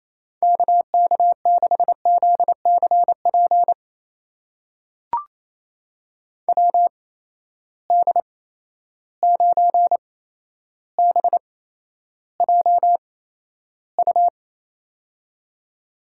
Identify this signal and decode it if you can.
Morse code, heard as a keyed tone, about 27 wpm, spaced as for 12 wpm, 700 Hz. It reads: KK6ZCP WD9BJU
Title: USA Call Signs - 8x Character Spacing - 28wpm